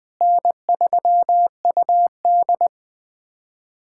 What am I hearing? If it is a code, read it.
Morse code, heard as a keyed tone, about 20 wpm, 700 Hz. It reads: N3UD
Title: POTA QSO Practice - 20wpm